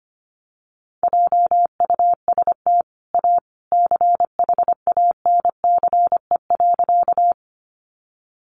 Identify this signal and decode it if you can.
Morse code, heard as a keyed tone, about 25 wpm, 700 Hz. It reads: JUST A CHANCE.
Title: Sentences from Top 800 Words 25wpm